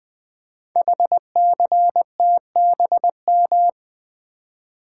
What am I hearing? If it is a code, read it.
Morse code, heard as a keyed tone, about 20 wpm, 700 Hz. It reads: HCTBM
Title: Code Groups - 5 Letters - 20wpm